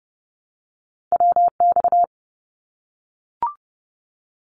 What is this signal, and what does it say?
Morse code, heard as a keyed tone, about 30 wpm, 700 Hz. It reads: WX
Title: Abbreviations 30wpm